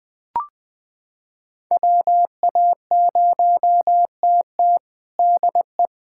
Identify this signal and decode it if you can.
Morse code, heard as a keyed tone, about 20 wpm, 700 Hz. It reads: WA0TT DE
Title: USA Callsign DE Callsign - 5 Characters - 20wpm